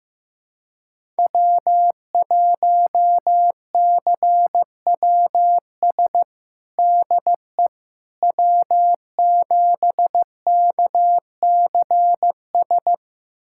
Morse code, heard as a keyed tone, about 15 wpm, 700 Hz. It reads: W1CWS DE W7KCS
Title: USA Callsign DE Callsign - 5 Characters - 15wpm